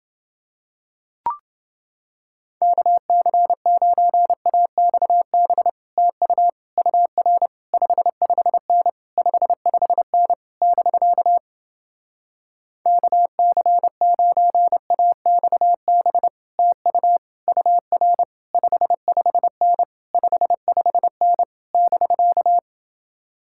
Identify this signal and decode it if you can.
Morse code, heard as a keyed tone, about 30 wpm, 700 Hz. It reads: KC9AXB TU UR 55N 55N <BK> KC9AXB TU UR 55N 55N <BK>